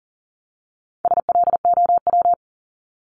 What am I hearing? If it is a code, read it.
Morse code, heard as a keyed tone, about 40 wpm, 700 Hz. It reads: SLOW
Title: LY Suffix - Top 100 Words with Base Words 40wpm